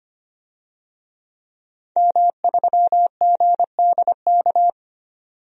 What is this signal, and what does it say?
Morse code, heard as a keyed tone, about 25 wpm, 700 Hz. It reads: M3GDK
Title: DXCC-100 Call Signs - 5 Characters - 25wpm